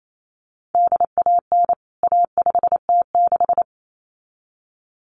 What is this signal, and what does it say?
Morse code, heard as a keyed tone, about 28 wpm, 700 Hz. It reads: DAN A5T6
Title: CWT Exchange 28wpm